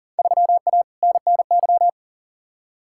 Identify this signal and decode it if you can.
Morse code, heard as a keyed tone, about 40 wpm, 700 Hz. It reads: 3A NNY
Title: Field Day Exchanges - 40wpm